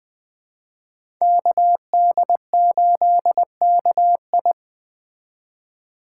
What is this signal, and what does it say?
Morse code, heard as a keyed tone, about 20 wpm, 700 Hz. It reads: KD8KI
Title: Speed Racing - USA 2x2 Call Signs 17wpm